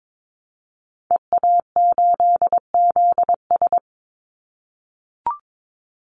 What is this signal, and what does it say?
Morse code, heard as a keyed tone, about 22 wpm, 700 Hz. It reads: EA8ZS
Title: Call Signs - 22wpm